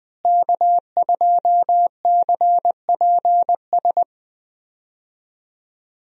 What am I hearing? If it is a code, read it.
Morse code, heard as a keyed tone, about 20 wpm, 700 Hz. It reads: K2CPS
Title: USA Call Signs - 5 Characters - Repeated 3x - 20wpm